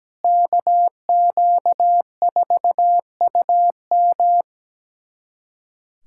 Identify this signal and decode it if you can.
Morse code, heard as a keyed tone, about 17 wpm, 700 Hz. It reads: KQ4UM
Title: USA Callsign DE Callsign - 5 Characters - 17wpm